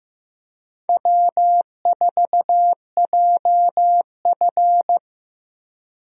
Morse code, heard as a keyed tone, about 16 wpm, 700 Hz. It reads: W4JF
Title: USA Call Signs - 4 Characters - Repeated 3x - 15wpm